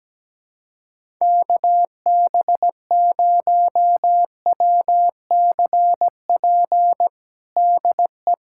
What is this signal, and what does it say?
Morse code, heard as a keyed tone, about 17 wpm, 700 Hz. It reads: KB0WCP DE
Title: USA Callsign DE Callsign - 6 Characters - 17wpm